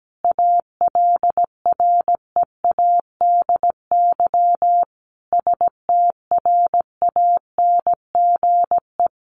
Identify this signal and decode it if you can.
Morse code, heard as a keyed tone, about 17 wpm, 700 Hz. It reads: ALREADY STRANGE
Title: A Sound Of Thunder 17wpm